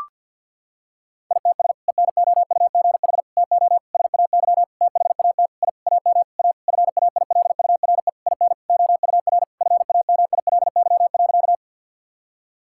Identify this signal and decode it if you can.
Morse code, heard as a keyed tone, about 51 wpm, 700 Hz. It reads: ITS ENOUGH TO SAY THAT I AM A FAILURE IN OUR FAMILY.
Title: Sentences from Top 1000 Words - Review Part 6 - 50wpm